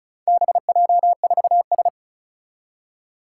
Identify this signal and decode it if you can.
Morse code, heard as a keyed tone, about 36 wpm, 700 Hz. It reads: BJ4S
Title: Four Challenging Letter-Number Combinations 35wpm